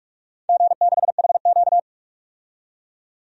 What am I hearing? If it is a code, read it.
Morse code, heard as a keyed tone, about 47 wpm, 700 Hz. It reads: G6HQ
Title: DXCC-100 Call Signs - 4 Characters - 45wpm